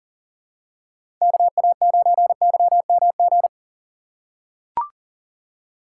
Morse code, heard as a keyed tone, about 40 wpm, 700 Hz. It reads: KA9YMG
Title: K1USN SST QSO Practice - 40wpm